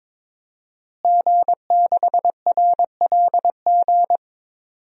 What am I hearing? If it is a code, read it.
Morse code, heard as a keyed tone, about 22 wpm, 700 Hz. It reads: G6RLG
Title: DXCC-100 Call Signs - 5 Characters - Repeated 3x - 22wpm